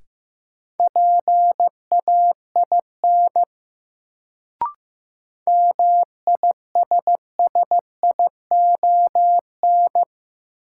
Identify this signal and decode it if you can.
Morse code, heard as a keyed tone, about 15 wpm, 700 Hz. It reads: PAIN MISSION
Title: Sentences from Top 1600 Words - New Word Drill - 15wpm